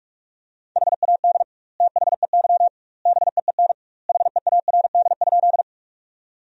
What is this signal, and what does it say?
Morse code, heard as a keyed tone, about 46 wpm, 700 Hz. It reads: HAD THEY BEEN HEARD?
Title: 2x Word-Spacing - Sentences from Top 400 Words - 45wpm